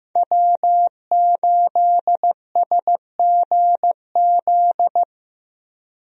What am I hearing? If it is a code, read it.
Morse code, heard as a keyed tone, about 15 wpm, 700 Hz. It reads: W8SGZ
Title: USA Call Signs - 5 Characters - 15wpm